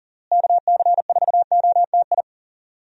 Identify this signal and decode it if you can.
Morse code, heard as a keyed tone, about 40 wpm, 700 Hz. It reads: KC4OTI
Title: USA Call Signs - 6 Characters - Repeated 2x - 40wpm